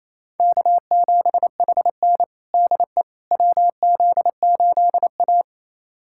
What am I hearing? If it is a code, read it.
Morse code, heard as a keyed tone, about 28 wpm, 700 Hz. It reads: K7HN DE WZ8A
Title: USA Callsign DE Callsign - 4 Characters - 28wpm